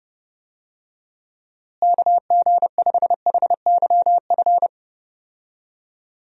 Morse code, heard as a keyed tone, about 32 wpm, 700 Hz. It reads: KG5HYF
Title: USA Call Signs - 6 Characters - Repeated 3x - 30wpm